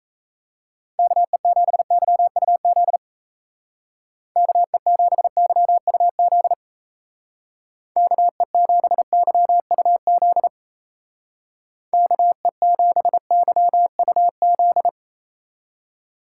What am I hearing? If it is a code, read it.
Morse code, heard as a keyed tone, about 33 wpm, 700 Hz. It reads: KE7YUZ KE7YUZ KE7YUZ KE7YUZ
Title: Speed Racing - USA 2x3 Call Signs 28wpm